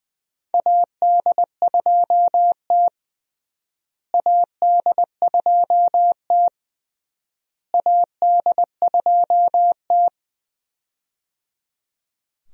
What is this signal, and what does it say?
Morse code, heard as a keyed tone, about 20 wpm, 700 Hz. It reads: AD2T AD2T AD2T
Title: USA Call Signs - 4 Characters - Repeated 3x - 20wpm